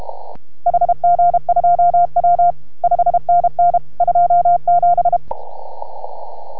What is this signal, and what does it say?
Morse code, heard as a keyed tone, about 32 wpm, 690 Hz. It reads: HG2W 5NN 27